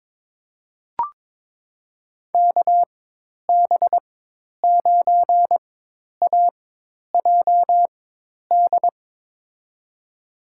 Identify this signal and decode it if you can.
Morse code, heard as a keyed tone, about 22 wpm, spaced as for 10 wpm, 700 Hz. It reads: KB9AJD
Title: USA Call Signs - 6 Characters - 4x Character Spacing - 22wpm